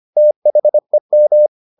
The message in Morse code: THEM

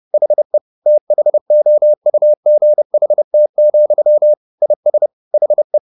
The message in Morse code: HE THOUGHT, IS HE

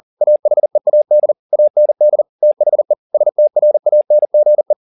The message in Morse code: AHEAD AND THE STRANGE

